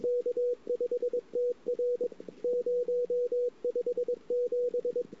Morse code, heard as a keyed tone, about 22 wpm, 490 Hz. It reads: K5TR 057
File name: CW / 2008-ss-cw-k5tr-op-n5rz -23:00